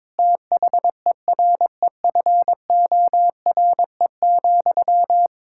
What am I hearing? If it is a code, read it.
Morse code, heard as a keyed tone, about 22 wpm, 700 Hz. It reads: THEREFORE,
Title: Sentences from Top 1000 Words - Review Part 5 - 22wpm